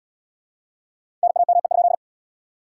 Morse code, heard as a keyed tone, about 78 wpm, 700 Hz. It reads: NAME?